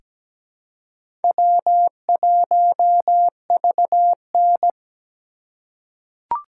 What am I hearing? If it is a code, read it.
Morse code, heard as a keyed tone, about 17 wpm, 700 Hz. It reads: W1VN
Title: USA Call Signs - 4 Characters - 2x Character Spacing - 17wpm